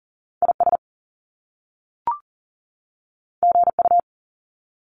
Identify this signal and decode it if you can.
Morse code, heard as a keyed tone, about 40 wpm, 700 Hz. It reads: IS GU
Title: Top Two Letter Combinations in Words 40wpm